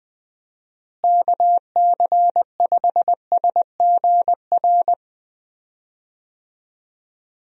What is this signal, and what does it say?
Morse code, heard as a keyed tone, about 20 wpm, 700 Hz. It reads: KC5SGR
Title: USA Call Signs - 6 Characters - Repeated 3x - 20wpm